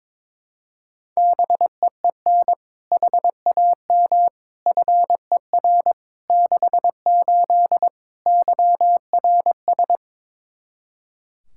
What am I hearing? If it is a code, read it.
Morse code, heard as a keyed tone, about 22 wpm, 700 Hz. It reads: BEEN HAM FER 68 YRS
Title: Practice QSO - Exchange 3 - More Info - 22wpm